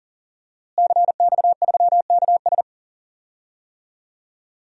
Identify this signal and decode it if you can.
Morse code, heard as a keyed tone, about 40 wpm, 700 Hz. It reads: CX3KS
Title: DXCC-100 Call Signs - 5 Characters - Repeated 2x - 40wpm